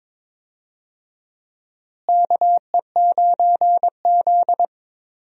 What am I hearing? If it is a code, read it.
Morse code, heard as a keyed tone, about 22 wpm, 700 Hz. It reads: KE9Z
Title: USA Call Signs - 4 Characters - Repeated 2x - 22wpm